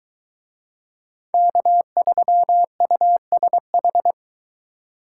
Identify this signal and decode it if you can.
Morse code, heard as a keyed tone, about 21 wpm, 700 Hz. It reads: K3USH